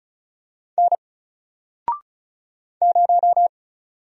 Morse code, heard as a keyed tone, about 36 wpm, 700 Hz. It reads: N 0
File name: Single-Letter-Number-35wpm